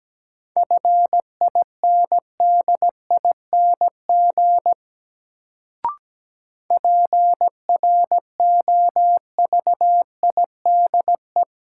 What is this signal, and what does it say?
Morse code, heard as a keyed tone, about 17 wpm, 700 Hz. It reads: FINDING PROVIDE
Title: Sentences from Top 1100 Words - New Word Drill Alt - 17wpm